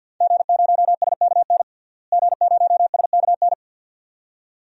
Morse code, heard as a keyed tone, about 52 wpm, 700 Hz. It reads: G0SKN G0SKN
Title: DXCC-100 Call Signs - 5 Characters - Repeated 2x - 50wpm